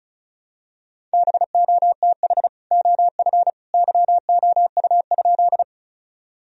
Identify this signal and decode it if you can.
Morse code, heard as a keyed tone, about 34 wpm, 700 Hz. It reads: BOTH OF YOU?
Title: Sentences from Top 1000 Words - Review Part 1 - 35wpm